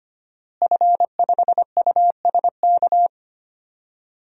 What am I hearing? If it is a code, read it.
Morse code, heard as a keyed tone, about 25 wpm, 700 Hz. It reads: F5USK
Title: DXCC-100 Call Signs - 5 Characters - Repeated 2x - 25wpm